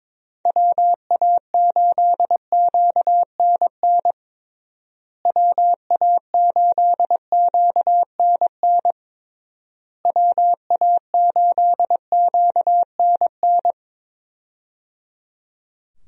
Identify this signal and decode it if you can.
Morse code, heard as a keyed tone, about 22 wpm, 700 Hz. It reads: WA8QNN WA8QNN WA8QNN
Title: USA Call Signs - 6 Characters - Repeated 3x - 22wpm